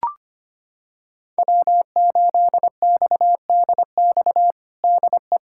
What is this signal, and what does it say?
Morse code, heard as a keyed tone, about 25 wpm, 700 Hz. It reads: W8XDX DE
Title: USA Callsign DE Callsign - 5 Characters - 25wpm